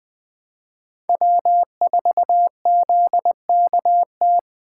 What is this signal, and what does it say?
Morse code, heard as a keyed tone, about 20 wpm, 700 Hz. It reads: W4ZKT
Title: USA Call Signs - 3x Character Spacing - 20wpm